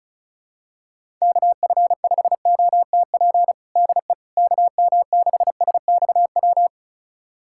Morse code, heard as a keyed tone, about 36 wpm, 700 Hz. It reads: KF5OTP DE KM6SXW
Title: USA Callsign DE Callsign 35wpm